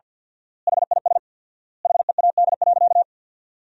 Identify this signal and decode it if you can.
Morse code, heard as a keyed tone, about 46 wpm, 700 Hz. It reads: HIS HEAD.
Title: 4x Word-Spacing - Sentences from Top 300 Words - 50wpm